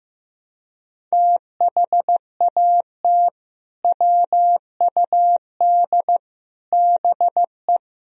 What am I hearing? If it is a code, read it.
Morse code, heard as a keyed tone, about 16 wpm, 700 Hz. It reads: THAT WUD BE
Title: Sentences from Top 500 Words and Abbreviations 15wpm